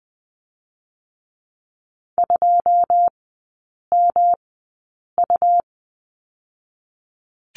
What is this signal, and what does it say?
Morse code, heard as a keyed tone, about 20 wpm, 700 Hz. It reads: 2 M U
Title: 107 Lesson 18e - Letter m - Three Characters 20wpm